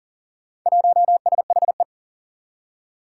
Morse code, heard as a keyed tone, about 38 wpm, 700 Hz. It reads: 1SHE
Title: Four Challenging Letter-Number Combinations 40wpm